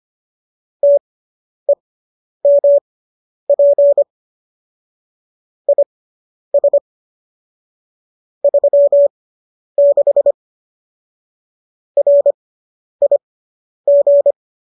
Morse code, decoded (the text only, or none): TEMP IS 36 RIG